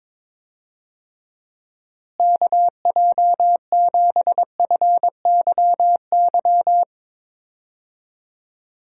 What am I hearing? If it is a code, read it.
Morse code, heard as a keyed tone, about 21 wpm, 700 Hz. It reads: KJ7FYY
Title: USA Call Signs - 6 Characters - Repeated 3x - 22wpm